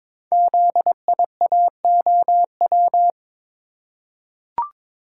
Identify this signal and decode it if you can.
Morse code, heard as a keyed tone, about 22 wpm, 700 Hz. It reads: ZIAOW
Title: Code Groups - 5 Letter:Numbers - 22wpm